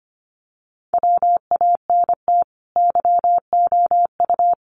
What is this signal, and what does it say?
Morse code, heard as a keyed tone, about 25 wpm, 700 Hz. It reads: WANT YOU
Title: Sets of 5 Words Encore 25wpm